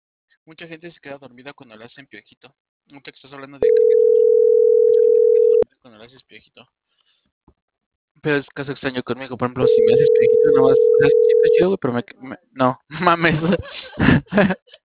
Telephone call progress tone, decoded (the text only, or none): ringback tone